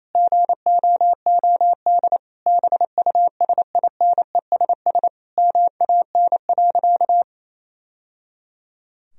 Morse code, heard as a keyed tone, about 27 wpm, 700 Hz. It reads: GOOD BUSINESS MAN.
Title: Sentences from Top 1000 Words - Review Part 4 - 28wpm